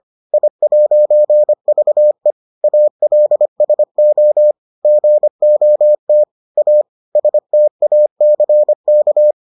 Morse code, heard as a keyed tone, about 25 wpm, 600 Hz. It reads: I'VE ALSO GOT A STACK